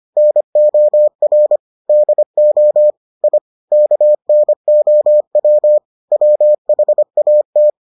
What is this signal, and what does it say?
Morse code, heard as a keyed tone, about 25 wpm, 600 Hz. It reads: NOR DO I KNOW WHAT